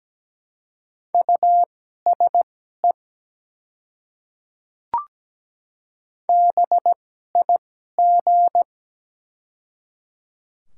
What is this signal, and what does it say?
Morse code, heard as a keyed tone, about 17 wpm, spaced as for 12 wpm, 700 Hz. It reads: USE BIG
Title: 2x Character Spacing - Top 200 Words - 17wpm